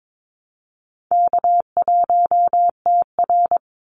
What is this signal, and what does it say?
Morse code, heard as a keyed tone, about 22 wpm, 700 Hz. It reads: K1TR
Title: Easy Call Signs - 22wpm